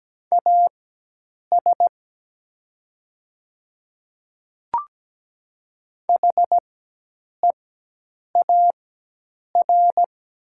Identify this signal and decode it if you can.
Morse code, heard as a keyed tone, about 17 wpm, spaced as for 8 wpm, 700 Hz. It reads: AS HEAR
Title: 4x Character Spacing - Top 500 Words - 17wpm